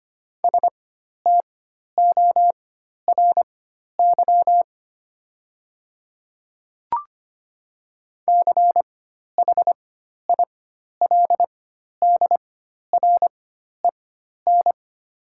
Morse code, heard as a keyed tone, about 25 wpm, spaced as for 12 wpm, 700 Hz. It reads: STORY CHILDREN
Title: 4x Character Spacing - Top 300 Words - 25wpm